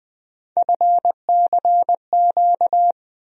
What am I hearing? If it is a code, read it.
Morse code, heard as a keyed tone, about 20 wpm, 700 Hz. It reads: FCQ
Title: USA Call Sign Postfixes - 3 Letters 20wpm